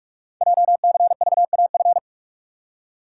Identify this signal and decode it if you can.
Morse code, heard as a keyed tone, about 44 wpm, 690 Hz. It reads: JCVAF
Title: Code Groups - 5 Letters - 45wpm